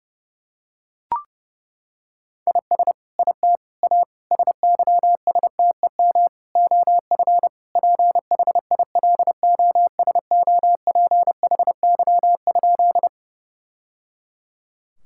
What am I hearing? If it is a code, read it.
Morse code, heard as a keyed tone, about 30 wpm, 700 Hz. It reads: IS IT A SYSTEM OF PHILOSOPHY?
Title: Sentences from Top 1000 Words - Review Part 6 - 30wpm